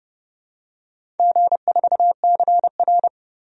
Morse code, heard as a keyed tone, about 30 wpm, 700 Hz. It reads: G4CR
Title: DXCC-100 Call Signs - 4 Characters - 30wpm